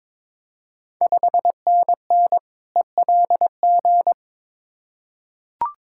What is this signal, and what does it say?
Morse code, heard as a keyed tone, about 22 wpm, 700 Hz. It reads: 5NN ELG
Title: Ontario QSO Party Exchange 22wpm